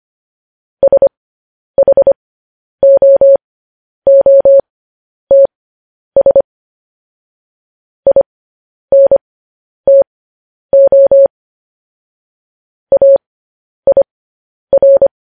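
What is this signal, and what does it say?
Morse code, heard as a keyed tone, about 25 wpm, spaced as for 10 wpm, 550 Hz. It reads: SHOOTS INTO AIR
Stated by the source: Morse code news headlines broadcast.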